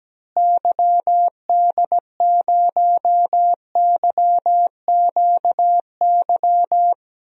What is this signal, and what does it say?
Morse code, heard as a keyed tone, about 17 wpm, 700 Hz. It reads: YD0YQY
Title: DXCC-100 Call Signs - 6 Characters - Repeated 2x - 17wpm